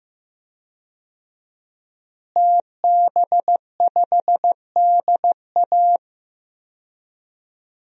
Morse code, heard as a keyed tone, about 15 wpm, 700 Hz. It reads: TB5DA
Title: DXCC-100 Call Signs - 5 Characters - Repeated 2x - 15wpm